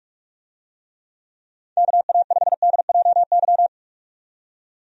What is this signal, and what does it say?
Morse code, heard as a keyed tone, about 46 wpm, 700 Hz. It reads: KA5DJY
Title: USA Call Signs - 6 Characters - 45wpm